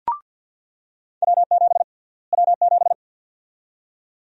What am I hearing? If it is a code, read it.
Morse code, heard as a keyed tone, about 48 wpm, 700 Hz. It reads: W7 W7
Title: USA Call Sign Prefixes - Repeated 2x - 50wpm